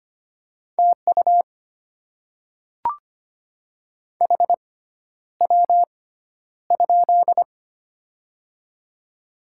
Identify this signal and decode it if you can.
Morse code, heard as a keyed tone, about 25 wpm, spaced as for 11 wpm, 700 Hz. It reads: TU HW?
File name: Top 20 QSO Elements - 6x Character Spacing - 25wpm